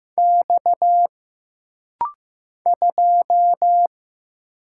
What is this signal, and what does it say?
Morse code, heard as a keyed tone, about 15 wpm, 700 Hz. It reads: X 2
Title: Single Letter-Number Alt - Rapid-Fire 15wpm